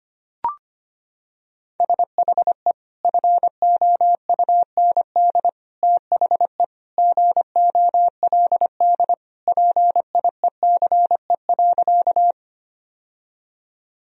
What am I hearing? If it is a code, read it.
Morse code, heard as a keyed tone, about 25 wpm, 700 Hz. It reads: SHE FOUND THE GOLD PIECE.
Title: Sentences from Top 1000 Words - Review Part 3 - 25wpm